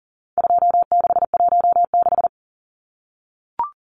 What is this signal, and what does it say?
Morse code, heard as a keyed tone, about 37 wpm, 700 Hz. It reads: 2616
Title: Numbers - 4-Digit 40wpm